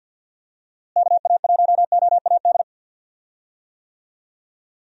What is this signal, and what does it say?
Morse code, heard as a keyed tone, about 51 wpm, 700 Hz. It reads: KA1OAD
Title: USA Call Signs - 6 Characters - Repeated 3x - 50wpm